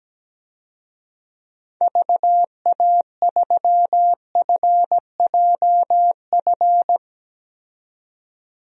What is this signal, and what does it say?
Morse code, heard as a keyed tone, about 17 wpm, 700 Hz. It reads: VA3FJF
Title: DXCC-100 Call Signs - 6 Characters - 17wpm